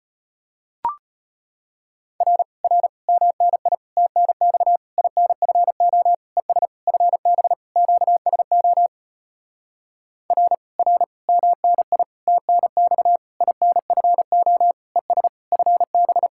Morse code, heard as a keyed tone, about 33 wpm, 700 Hz. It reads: R R MNI TNX INFO ES FB QSO R R MNI TNX INFO ES FB